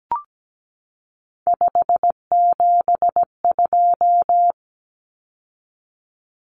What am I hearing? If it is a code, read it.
Morse code, heard as a keyed tone, about 17 wpm, 700 Hz. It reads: 572
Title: Common Cut Numbers - 3-Digit 17wpm